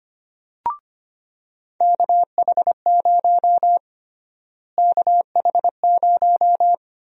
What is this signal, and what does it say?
Morse code, heard as a keyed tone, about 25 wpm, 700 Hz. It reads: KH0 KH0
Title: USA Call Sign Prefixes - Repeated 2x - 25wpm